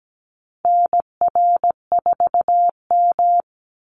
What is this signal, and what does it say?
Morse code, heard as a keyed tone, about 17 wpm, 700 Hz. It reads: NR4M
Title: Call Signs Repeated 2x - 17wpm